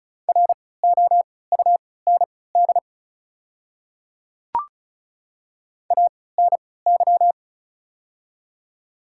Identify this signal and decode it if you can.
Morse code, heard as a keyed tone, about 36 wpm, spaced as for 20 wpm, 700 Hz. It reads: ROUND ANY
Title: 3x Character Spacing - Top 300 Words - 35wpm